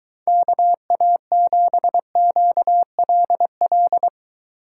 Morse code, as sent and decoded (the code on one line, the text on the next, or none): -.- .- --... --.- .-.. .-..
KA7QLL